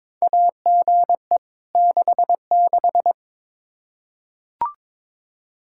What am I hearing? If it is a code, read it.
Morse code, heard as a keyed tone, about 22 wpm, 700 Hz. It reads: AGE 66
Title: K1USN SST QSO Practice 22wpm